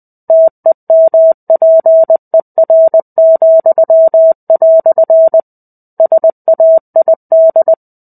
Morse code, heard as a keyed tone, about 20 wpm, 650 Hz. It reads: TEMPER," SAID
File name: 05-Chapter-05 20 WPM